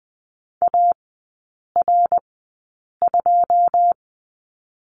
Morse code, heard as a keyed tone, about 20 wpm, spaced as for 8 wpm, 690 Hz. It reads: AR2
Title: 093 Lesson 16e - Letter c - Three Characters 20wpm